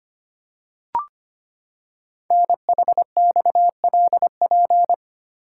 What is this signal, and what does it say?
Morse code, heard as a keyed tone, about 25 wpm, 700 Hz. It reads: NHXLP
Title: Code Groups - 5 Letter:Numbers - 25wpm